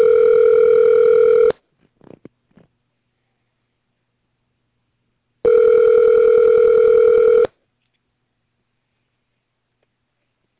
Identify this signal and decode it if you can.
Ringback tone (North America), 440 + 480 Hz.